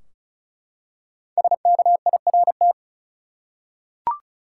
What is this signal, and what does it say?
Morse code, heard as a keyed tone, about 36 wpm, 700 Hz. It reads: SKIRT
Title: Five Letter Words - 35wpm